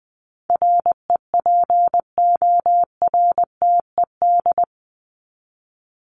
Morse code, heard as a keyed tone, about 20 wpm, 700 Hz. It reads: REPORTED
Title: ED Suffix - Top 100 Words with Base Words 20wpm